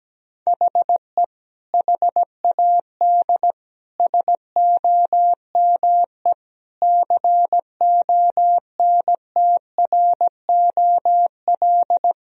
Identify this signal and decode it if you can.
Morse code, heard as a keyed tone, about 17 wpm, 700 Hz. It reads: HE HAD SOME CONTROL
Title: Sentences from Top 1000 Words - Review Part 3 - 17wpm